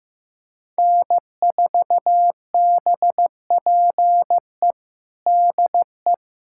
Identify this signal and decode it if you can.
Morse code, heard as a keyed tone, about 16 wpm, 700 Hz. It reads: N4BPE DE
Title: USA Callsign DE Callsign 15wpm